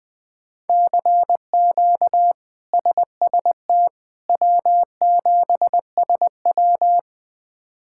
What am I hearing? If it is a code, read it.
Morse code, heard as a keyed tone, about 20 wpm, 700 Hz. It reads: CQ SST W7SW
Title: K1USN SST QSO Practice - 20wpm